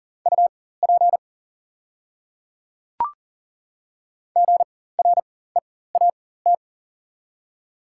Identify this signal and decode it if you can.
Morse code, heard as a keyed tone, about 40 wpm, spaced as for 19 wpm, 700 Hz. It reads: UP GREAT
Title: 4x Character Spacing - Top 200 Words - 40wpm